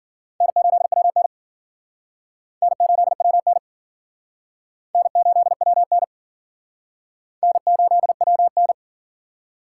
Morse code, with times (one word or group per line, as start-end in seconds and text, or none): 0.4-1.3 N8WN
2.6-3.6 N8WN
5.0-6.0 N8WN
7.4-8.7 N8WN